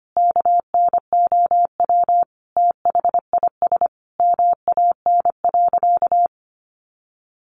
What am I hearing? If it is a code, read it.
Morse code, heard as a keyed tone, about 25 wpm, 700 Hz. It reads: KNOW THIS MAN.
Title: Sentences from Top 600 Words 25wpm